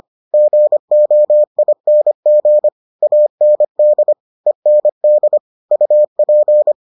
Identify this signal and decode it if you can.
Morse code, heard as a keyed tone, about 25 wpm, 600 Hz. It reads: GOING AND END UP